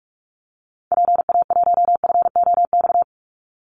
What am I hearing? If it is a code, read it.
Morse code, heard as a keyed tone, about 44 wpm, 700 Hz. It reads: PA1FOX
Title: Call Signs - 45wpm